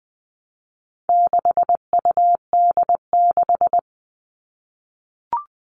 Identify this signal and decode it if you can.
Morse code, heard as a keyed tone, about 20 wpm, 700 Hz. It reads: 6UD6
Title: All Cut Numbers - 4-Digit 20wpm